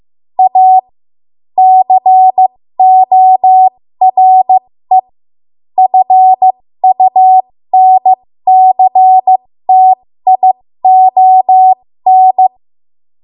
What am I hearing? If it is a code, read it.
Morse code, heard as a keyed tone, about 15 wpm, spaced as for 13 wpm, 750 Hz. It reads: A CORE FUNCTION
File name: Morse / 13WPM